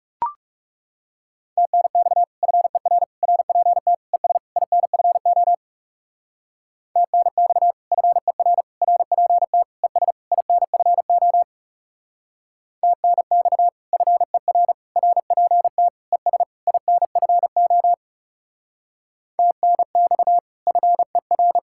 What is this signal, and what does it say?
Morse code, heard as a keyed tone, about 37 wpm, 700 Hz. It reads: TNX FER RPT ES INFO TNX FER RPT ES INFO TNX FER RPT ES INFO TNX FER
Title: Speed-Racing Common QSO Phrases - 30wpm